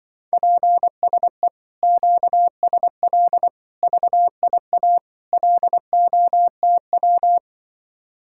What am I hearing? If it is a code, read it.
Morse code, heard as a keyed tone, about 24 wpm, 700 Hz. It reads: PSE QSL VIA LOTW
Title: Speed-Racing Common QSO Phrases - 20wpm